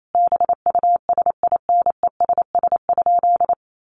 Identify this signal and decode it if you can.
Morse code, heard as a keyed tone, about 28 wpm, 700 Hz. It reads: BUSINESS?
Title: Sentences from Top 1400 Words 28wpm